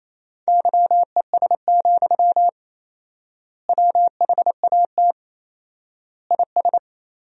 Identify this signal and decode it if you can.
Morse code, heard as a keyed tone, about 28 wpm, 700 Hz. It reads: YES, WHAT IS
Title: 4x Word-Spacing - Sentences from Top 500 Words - 28wpm